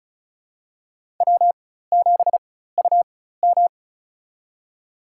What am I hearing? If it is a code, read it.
Morse code, heard as a keyed tone, about 36 wpm, spaced as for 16 wpm, 700 Hz. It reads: W7UM
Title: USA Call Signs - 4x Character Spacing - 35wpm